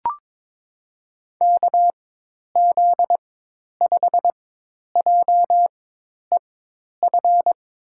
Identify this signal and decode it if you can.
Morse code, heard as a keyed tone, about 21 wpm, spaced as for 10 wpm, 700 Hz. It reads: KZ5JEF
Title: USA Call Signs - 6 Characters - 4x Character Spacing - 22wpm